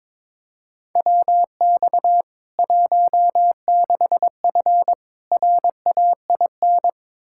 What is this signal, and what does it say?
Morse code, heard as a keyed tone, about 22 wpm, 700 Hz. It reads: WX 16F RAIN
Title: K1USN SST QSO Practice 22wpm